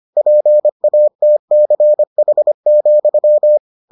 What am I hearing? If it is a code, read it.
Morse code, heard as a keyed tone, about 25 wpm, 600 Hz. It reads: PATCH,